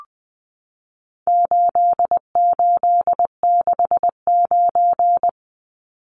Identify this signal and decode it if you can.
Morse code, heard as a keyed tone, about 20 wpm, 700 Hz. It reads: 8869
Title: Numbers - 4-Digit 20wpm